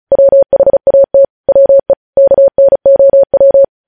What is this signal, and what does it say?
Morse code, heard as a keyed tone, about 36 wpm, 550 Hz. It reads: WHAT WE KNOW
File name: podcast / News headlines-1x-35x35